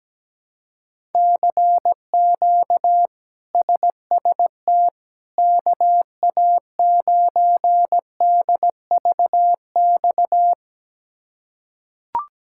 Morse code, heard as a keyed tone, about 17 wpm, 700 Hz. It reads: CQ SST KA9DVX
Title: K1USN SST QSO Practice - 17wpm